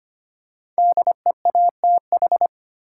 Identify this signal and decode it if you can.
Morse code, heard as a keyed tone, about 25 wpm, 700 Hz. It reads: DEATH
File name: 5 Letter Words - First 3 Letters Given - 25wpm